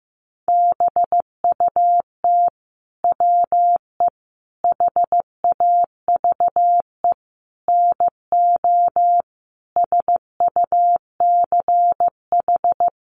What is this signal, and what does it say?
Morse code, heard as a keyed tone, about 15 wpm, 700 Hz. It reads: BUT WE HAVE NO SUCH
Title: Sentences from Top 1300 Words 15wpm